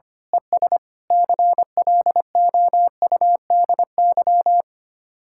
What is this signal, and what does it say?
Morse code, heard as a keyed tone, about 24 wpm, 700 Hz. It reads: ES CLOUDY
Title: Speed-Racing Common QSO Phrases - 25wpm